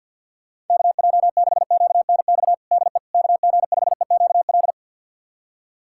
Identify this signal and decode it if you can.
Morse code, heard as a keyed tone, about 47 wpm, 700 Hz. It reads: KJ6QNX DE KG5EQL
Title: USA Callsign DE Callsign - 6 Characters - 50wpm